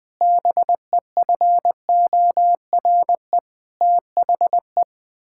Morse code, heard as a keyed tone, about 19 wpm, 700 Hz. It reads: BEFORE THE
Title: Sets of 9 Words - 20wpm